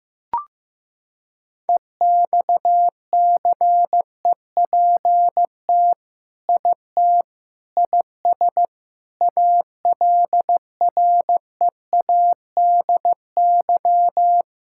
Morse code, heard as a keyed tone, about 15 wpm, 700 Hz. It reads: EXCEPT IT IS ALREADY
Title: Sentences from Top 1600 Words 15wpm